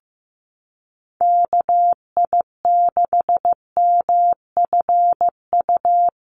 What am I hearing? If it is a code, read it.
Morse code, heard as a keyed tone, about 15 wpm, 700 Hz. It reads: KI6MFU
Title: USA Call Signs Repeated 3x 15wpm